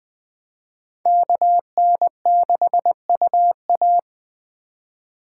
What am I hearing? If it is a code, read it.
Morse code, heard as a keyed tone, about 20 wpm, 700 Hz. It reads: KN6UA
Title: Speed Racing - USA Call Signs 17wpm